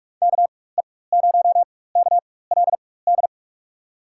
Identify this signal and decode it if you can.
Morse code, heard as a keyed tone, about 47 wpm, spaced as for 21 wpm, 700 Hz. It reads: KE0KLD
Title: USA Call Signs - 6 Characters - 4x Character Spacing - 45wpm